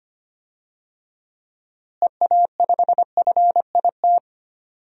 Morse code, heard as a keyed tone, about 25 wpm, 700 Hz. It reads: EA5FIT
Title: DXCC-100 Call Signs - 6 Characters - Repeated 3x - 25wpm